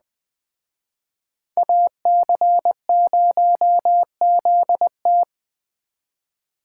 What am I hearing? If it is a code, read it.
Morse code, heard as a keyed tone, about 20 wpm, 700 Hz. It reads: AC0ZT
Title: Speed Racing - USA 2x2 Call Signs 17wpm